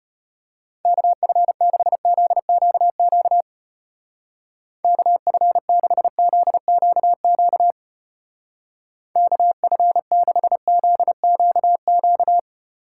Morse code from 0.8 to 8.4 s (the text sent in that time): KF6ZQQ KF6ZQQ